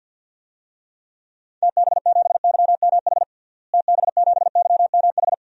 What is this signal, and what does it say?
Morse code, heard as a keyed tone, about 51 wpm, 700 Hz. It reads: TB7YMH TB7YMH
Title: DXCC-100 Call Signs - 6 Characters - Repeated 3x - 50wpm